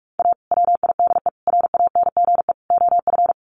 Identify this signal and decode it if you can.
Morse code, heard as a keyed tone, about 44 wpm, 700 Hz. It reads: A WIDE RANGE OF
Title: Sets of 4 Words 45wpm